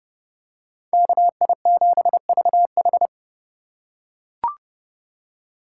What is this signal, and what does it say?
Morse code, heard as a keyed tone, about 28 wpm, 700 Hz. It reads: KI7VH